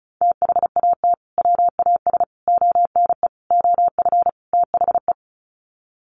Morse code, heard as a keyed tone, about 34 wpm, 700 Hz. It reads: THAT WAS ONE OF THE
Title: Sets of 5 Words 35wpm